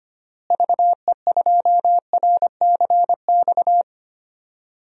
Morse code, heard as a keyed tone, about 25 wpm, 700 Hz. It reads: VE2RCX
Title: DXCC-100 Call Signs - 6 Characters - Repeated 3x - 25wpm